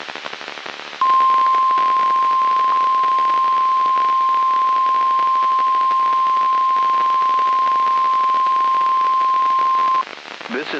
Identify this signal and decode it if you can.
Emergency Alert System (SAME) tones (the 1050 Hz attention tone), with speech. It is heard from 1.0 to 10.1 s.
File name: eas messages / EAS WXM95 Towanda